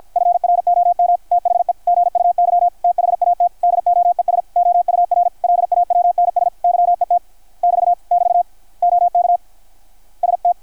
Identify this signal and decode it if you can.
Morse code, heard as a keyed tone, about 53 wpm, 690 Hz. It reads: FROM THE GUY THAT DOES OUR LAWNS YET <BT> <BT> OK ST
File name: archived / CRSnet